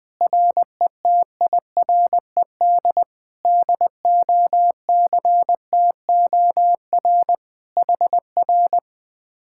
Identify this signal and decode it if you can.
Morse code, heard as a keyed tone, about 20 wpm, 700 Hz. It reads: RETIRED DOCTOR HR
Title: Speed-Racing Common QSO Phrases - 20wpm